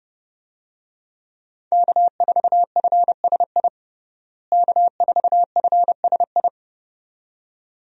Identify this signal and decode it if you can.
Morse code, heard as a keyed tone, about 30 wpm, 700 Hz. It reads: K4FSI K4FSI
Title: USA Call Signs - 5 Characters - Repeated 2x - 30wpm